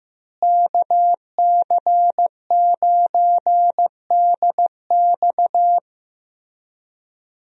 Morse code, heard as a keyed tone, about 15 wpm, 700 Hz. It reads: KC9DX
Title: USA Call Signs - 5 Characters - Repeated 2x - 15wpm